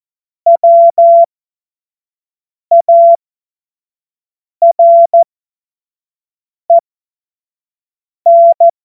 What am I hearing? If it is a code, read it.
Morse code, heard as a keyed tone, about 14 wpm, spaced as for 5 wpm, 680 Hz. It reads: WAREN
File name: Morsekurs / Witze / Witz 27. Dezember 2022 25 F70